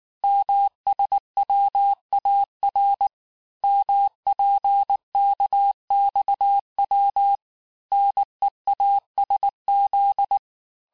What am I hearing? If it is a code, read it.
Morse code, heard as a keyed tone, about 19 wpm, 800 Hz. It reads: MSWAR MPKXW NEASZ
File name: cours-du-lundi files / cours-1583--texte-chiffre-18-mpm